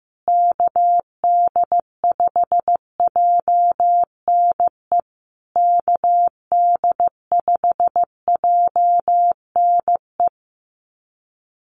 Morse code, heard as a keyed tone, about 15 wpm, 700 Hz. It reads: KD5JNE KD5JNE
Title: USA Call Signs Repeated 3x 15wpm